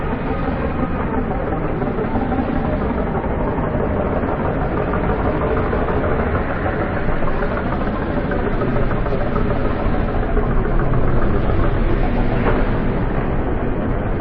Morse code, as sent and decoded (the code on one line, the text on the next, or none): none